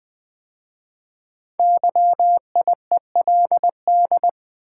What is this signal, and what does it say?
Morse code, heard as a keyed tone, about 20 wpm, 700 Hz. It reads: YIELD
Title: 5 Letter Words - First 3 Letters Given - 20wpm